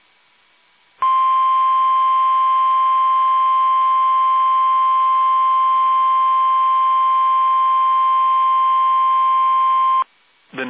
Emergency Alert System (SAME) tones (the 1050 Hz attention tone). It is heard from 1.0 to 10.1 s.